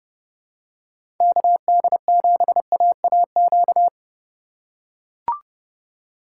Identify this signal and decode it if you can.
Morse code, heard as a keyed tone, about 31 wpm, 700 Hz. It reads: KD7AAQ
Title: USA Call Signs - 6 Characters - 2x Character Spacing - 30wpm